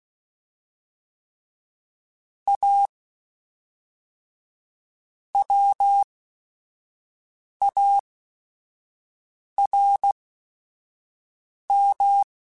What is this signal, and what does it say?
Morse code, heard as a keyed tone, about 16 wpm, spaced as for 5 wpm, 800 Hz. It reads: A WARM
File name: Weather Sets / Weather Sets-mp3s / Weather 4 28 @5